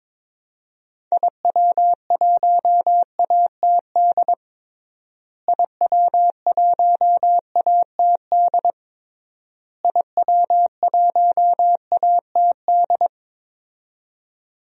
Morse code, heard as a keyed tone, about 21 wpm, 700 Hz. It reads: IW1ATD IW1ATD IW1ATD
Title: DXCC-100 Call Signs - 6 Characters - Repeated 3x - 22wpm